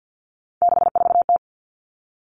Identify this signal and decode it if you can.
Morse code, heard as a keyed tone, about 50 wpm, 700 Hz. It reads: 64T
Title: Common Cut Numbers - 3-Digit 50wpm